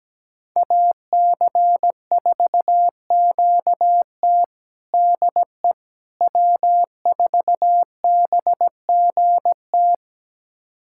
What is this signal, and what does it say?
Morse code, heard as a keyed tone, about 17 wpm, 700 Hz. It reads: AC4QT DE W4BGT
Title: USA Callsign DE Callsign - 5 Characters - 17wpm